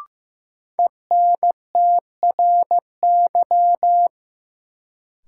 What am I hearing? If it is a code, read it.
Morse code, heard as a keyed tone, about 15 wpm, 700 Hz. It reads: ENTRY
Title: Sentences from Top 1500 Words - New Word Drill Alt - 15wpm